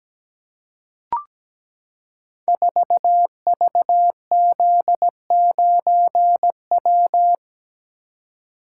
Morse code, heard as a keyed tone, about 17 wpm, 700 Hz. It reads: 4VZ9W
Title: Code Groups - 5 Letter:Numbers - 17wpm